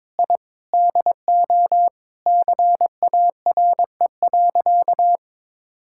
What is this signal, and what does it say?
Morse code, heard as a keyed tone, about 22 wpm, 700 Hz. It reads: I DO CARE.
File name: practice / 22wpm / Sentences with 3-Word Phrases - 22wpm